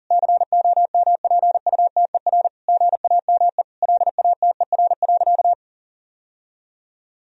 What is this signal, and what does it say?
Morse code, heard as a keyed tone, about 40 wpm, 700 Hz. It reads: COMPUTER GAME LATER.
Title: Sentences from Top 1000 Words - Review Part 3 - 40wpm